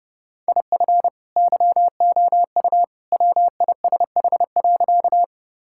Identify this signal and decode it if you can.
Morse code, heard as a keyed tone, about 28 wpm, 700 Hz. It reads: IF YOU WISH.
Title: Sentences from Top 1600 Words 30wpm